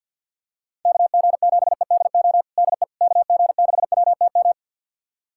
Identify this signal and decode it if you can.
Morse code, heard as a keyed tone, about 46 wpm, 700 Hz. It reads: KG7EDO DE KG6WTM
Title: USA Callsign DE Callsign 50wpm